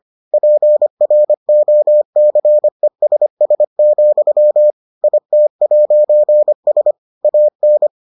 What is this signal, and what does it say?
Morse code, heard as a keyed tone, about 25 wpm, 600 Hz. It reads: PROCESS, IT'S AN